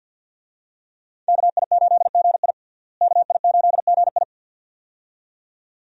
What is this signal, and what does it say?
Morse code, heard as a keyed tone, about 52 wpm, 700 Hz. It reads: KI8GI KI8GI
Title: USA Call Signs - 5 Characters - Repeated 2x - 50wpm